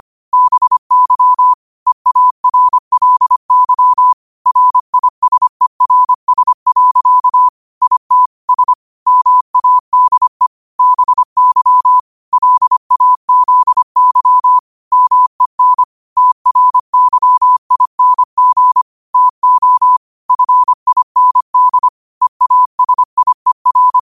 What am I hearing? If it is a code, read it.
Morse code, heard as a keyed tone, about 25 wpm, 1000 Hz. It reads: BY EARLY RISERS. IT S MADE BY LAZY MEN TRYING TO FIND EASIER
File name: qotdfiles / 25wpm